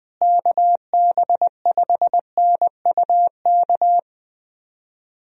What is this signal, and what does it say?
Morse code, heard as a keyed tone, about 20 wpm, 700 Hz. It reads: KB5NUK
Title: USA Call Signs - 6 Characters - Repeated 2x - 20wpm